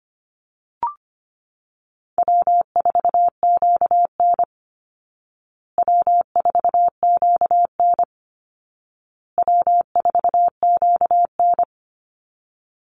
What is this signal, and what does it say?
Morse code, heard as a keyed tone, about 25 wpm, 700 Hz. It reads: W4QN W4QN W4QN
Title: Easy Call Signs Repeated 3x - 25wpm